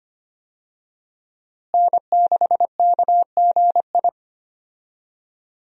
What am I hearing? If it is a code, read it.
Morse code, heard as a keyed tone, about 25 wpm, 700 Hz. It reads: N6KGI
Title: USA Call Signs - 5 Characters - Repeated 2x - 25wpm